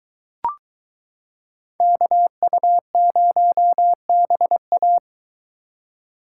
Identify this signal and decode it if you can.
Morse code, heard as a keyed tone, about 23 wpm, 700 Hz. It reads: KU0BA